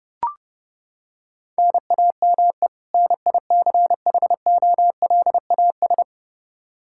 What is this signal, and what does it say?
Morse code, heard as a keyed tone, about 32 wpm, 700 Hz. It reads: NAME NICHOLAS
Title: K1USN SST QSO Practice 30wpm